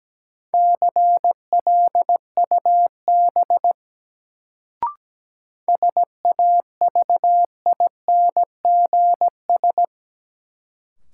Morse code, heard as a keyed tone, about 17 wpm, 700 Hz. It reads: CLUB SAVINGS
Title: Sentences from Top 1200 Words - New Word Drill - 17wpm